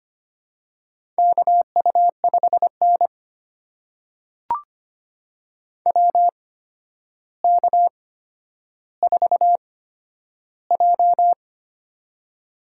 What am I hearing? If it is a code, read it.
Morse code, heard as a keyed tone, about 25 wpm, spaced as for 11 wpm, 700 Hz. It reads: KU5N WK4J
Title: USA Call Signs - 4 Characters - 8x Character Spacing - 25wpm